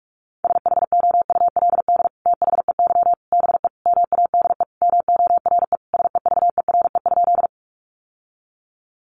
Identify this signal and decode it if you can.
Morse code, heard as a keyed tone, about 44 wpm, 700 Hz. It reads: SHOULD THEY BE MADE MORE SEVERE?
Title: Sentences from Top 1400 Words 45wpm